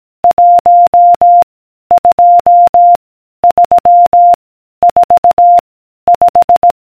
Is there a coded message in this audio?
Yes: Morse code from 0.2 to 6.7 s, heard as a keyed tone, about 18 wpm, 700 Hz. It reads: 1 2 3 4 5